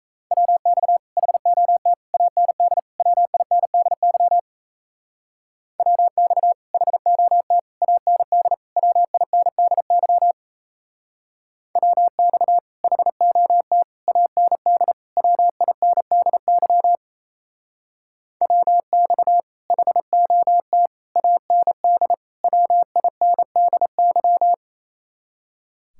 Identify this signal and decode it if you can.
Morse code, heard as a keyed tone, about 34 wpm, 700 Hz. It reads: WX HOT AND WINDY WX HOT AND WINDY WX HOT AND WINDY WX HOT AND WINDY
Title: Speed-Racing Common QSO Phrases - 28wpm